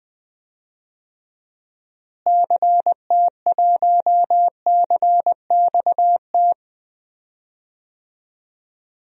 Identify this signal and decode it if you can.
Morse code, heard as a keyed tone, about 20 wpm, 700 Hz. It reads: CT1CXT
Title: DXCC-100 Call Signs - 6 Characters - Repeated 3x - 20wpm